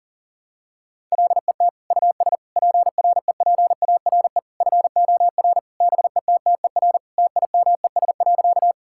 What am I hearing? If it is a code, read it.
Morse code, heard as a keyed tone, about 37 wpm, 700 Hz. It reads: LET US PREPARE FOR BETTER TIMES.
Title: Sentences from Top 1000 Words - Review Part 6 - 40wpm